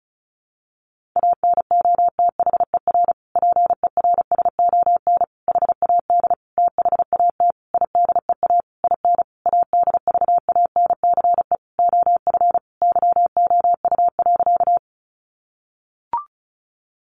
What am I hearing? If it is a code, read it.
Morse code, heard as a keyed tone, about 34 wpm, 700 Hz. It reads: ANOTHER PERSON HAD THAT IDEA IN ADVANCE OF YOU.
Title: Sentences from Top 1400 Words 35wpm